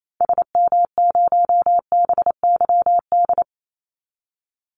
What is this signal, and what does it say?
Morse code, heard as a keyed tone, about 28 wpm, 700 Hz. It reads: SM0BYD
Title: Call Signs Repeated 2x - 28wpm